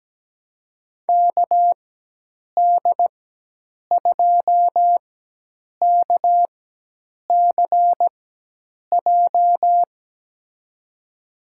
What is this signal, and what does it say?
Morse code, heard as a keyed tone, about 17 wpm, spaced as for 8 wpm, 700 Hz. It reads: KD2KCJ
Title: USA Call Signs - 4x Character Spacing - 17wpm